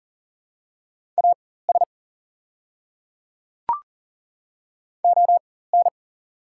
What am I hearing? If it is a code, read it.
Morse code, heard as a keyed tone, about 39 wpm, spaced as for 18 wpm, 690 Hz. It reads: AS ON
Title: 4x Character Spacing - Top 100 Words - 40wpm